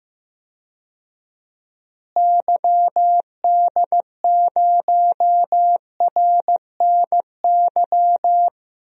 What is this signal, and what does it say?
Morse code, heard as a keyed tone, about 15 wpm, 700 Hz. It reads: YD0RNY
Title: DXCC-100 Call Signs - 6 Characters - Repeated 3x - 15wpm